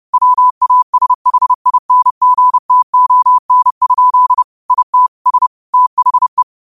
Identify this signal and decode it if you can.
Morse code, heard as a keyed tone, about 30 wpm, 1000 Hz. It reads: WASHINGTON? IT S THE
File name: qotdfiles / 30wpm